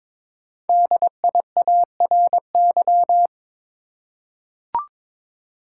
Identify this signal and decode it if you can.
Morse code, heard as a keyed tone, about 23 wpm, 700 Hz. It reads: DIARY
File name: 5 Letter Words - First 3 Letters Given - 22wpm